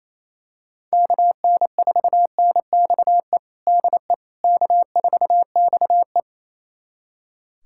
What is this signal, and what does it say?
Morse code, heard as a keyed tone, about 28 wpm, 700 Hz. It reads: KN4NXE DE K4XE
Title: USA Callsign DE Callsign 28wpm